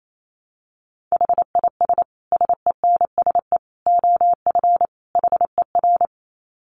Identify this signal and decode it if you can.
Morse code, heard as a keyed tone, about 28 wpm, 700 Hz. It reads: HIS SENSE OF HER
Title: Dit Challenge 28wpm